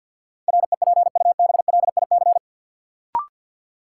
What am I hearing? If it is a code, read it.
Morse code, heard as a keyed tone, about 52 wpm, 700 Hz. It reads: REPUBLIC
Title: Sentences from Top 1500 Words - New Word Drill Alt - 50wpm